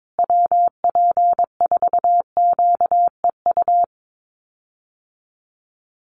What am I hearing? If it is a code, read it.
Morse code, heard as a keyed tone, about 22 wpm, 700 Hz. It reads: WP4QEU
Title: USA Call Signs Repeated 3x 22wpm